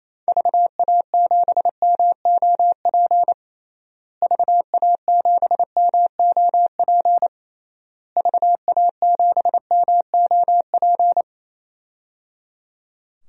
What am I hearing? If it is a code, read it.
Morse code, heard as a keyed tone, about 27 wpm, 700 Hz. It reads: VA7MOP VA7MOP VA7MOP
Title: DXCC-100 Call Signs - 6 Characters - Repeated 3x - 28wpm